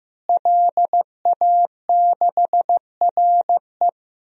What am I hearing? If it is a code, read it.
Morse code, heard as a keyed tone, about 15 wpm, 700 Hz. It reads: LA6RE
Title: DXCC-100 Call Signs - 5 Characters - Repeated 3x - 15wpm